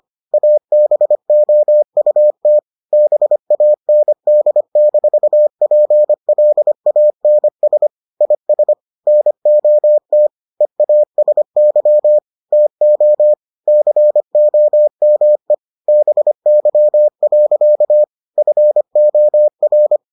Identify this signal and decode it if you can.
Morse code, heard as a keyed tone, about 25 wpm, 600 Hz. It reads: ABOUT BAND-PLANS IS NOT EASY TO COME BY. FOR